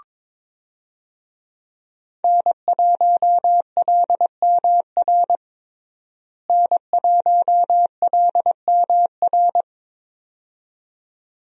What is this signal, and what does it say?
Morse code, heard as a keyed tone, about 22 wpm, 700 Hz. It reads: N1LMR N1LMR
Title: USA Call Signs - 5 Characters - Repeated 2x - 22wpm